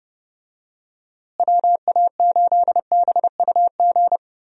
Morse code, heard as a keyed tone, about 31 wpm, 700 Hz. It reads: WA8BUG
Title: USA Call Signs - 6 Characters - Repeated 2x - 30wpm